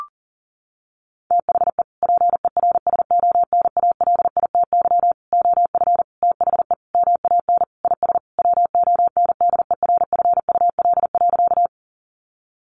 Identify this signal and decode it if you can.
Morse code, heard as a keyed tone, about 40 wpm, 700 Hz. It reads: THE PERSONALITY OF THE MAN IS WONDERFUL.
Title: Sentences from Top 1400 Words 40wpm